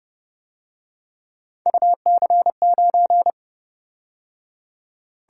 Morse code, heard as a keyed tone, about 28 wpm, 700 Hz. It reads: UC9